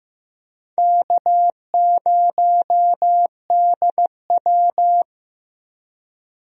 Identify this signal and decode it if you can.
Morse code, heard as a keyed tone, about 15 wpm, 700 Hz. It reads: K0DW
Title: USA Call Signs - 4 Characters - Repeated 3x - 15wpm